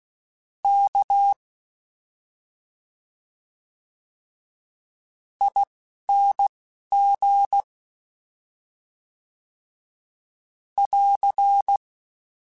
Morse code, heard as a keyed tone, about 16 wpm, 800 Hz. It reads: K I N G <AR>
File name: Special INGs-mp3s / Special INGs-mp3s / Slow ing @10